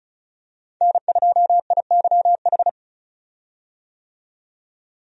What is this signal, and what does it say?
Morse code, heard as a keyed tone, about 36 wpm, 700 Hz. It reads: N2IYH
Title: USA Call Signs - 5 Characters - 35wpm